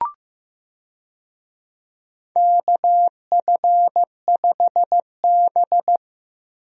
Morse code, heard as a keyed tone, about 15 wpm, 700 Hz. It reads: KF5B
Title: USA Call Signs - 4 Characters - Repeated 2x - 15wpm